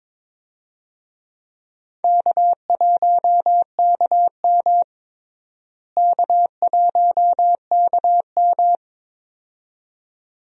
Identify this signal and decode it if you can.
Morse code, heard as a keyed tone, about 22 wpm, 700 Hz. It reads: K1KM K1KM
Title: USA Call Signs - 4 Characters - Repeated 2x - 22wpm